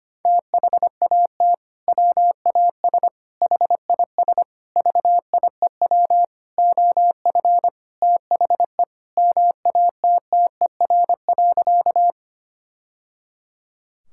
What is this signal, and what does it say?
Morse code, heard as a keyed tone, about 25 wpm, 700 Hz. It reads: THAT WAS HIS VIEW OF THE MATTER.
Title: Sentences from Top 1000 Words - Review Part 6 - 25wpm